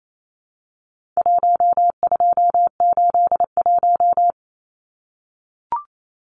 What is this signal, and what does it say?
Morse code, heard as a keyed tone, about 28 wpm, 700 Hz. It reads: 1281
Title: Numbers - 4-Digit 28wpm